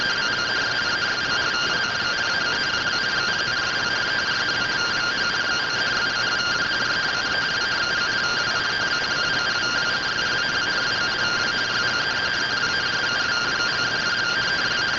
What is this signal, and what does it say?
Two-tone teleprinter signal, about 170 Hz shift.
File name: findings / SAGITAL fr15 nb203 3mm.gen-30 rtty rx